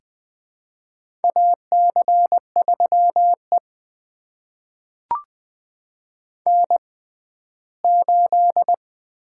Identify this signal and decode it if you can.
Morse code, heard as a keyed tone, about 21 wpm, spaced as for 14 wpm, 700 Hz. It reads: AC3E N8
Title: USA Call Signs - 4 Characters - 6x Character Spacing - 20wpm